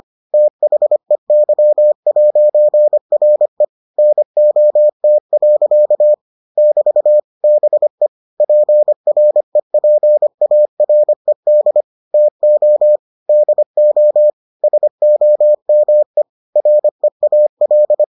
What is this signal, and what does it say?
Morse code, heard as a keyed tone, about 25 wpm, 600 Hz. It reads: THEY'RE NOT. <BT> BE PREPARED TO DO SOME REAL